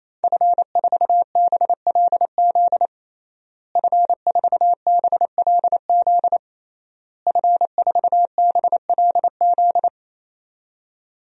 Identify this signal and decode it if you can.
Morse code, heard as a keyed tone, about 28 wpm, 700 Hz. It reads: F4BLZ F4BLZ F4BLZ
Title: DXCC-100 Call Signs - 5 Characters - Repeated 3x - 28wpm